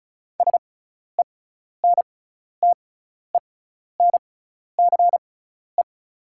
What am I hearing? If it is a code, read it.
Morse code, heard as a keyed tone, about 36 wpm, spaced as for 12 wpm, 700 Hz. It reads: SENTENCE
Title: 6x Character Spacing - Top 200 Words - 35wpm